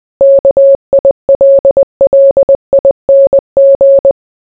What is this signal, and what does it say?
Morse code, heard as a keyed tone, about 20 wpm, 550 Hz. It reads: KILLING